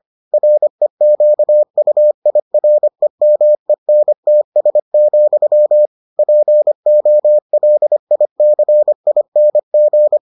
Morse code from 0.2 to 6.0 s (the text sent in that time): REQUIREMENTS,